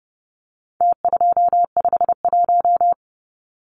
Morse code, heard as a keyed tone, about 30 wpm, 700 Hz. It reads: T251
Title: Numbers - Contest Style - 4-Digit 30wpm